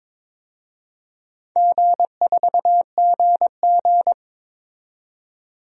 Morse code, heard as a keyed tone, about 22 wpm, 700 Hz. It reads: G4GG
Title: DXCC-100 Call Signs - 4 Characters - 22wpm